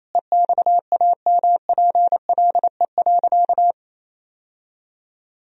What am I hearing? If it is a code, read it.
Morse code, heard as a keyed tone, about 28 wpm, 700 Hz. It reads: EXAMPLE.
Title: Sentences from Top 1000 Words - Review Part 1 - 28wpm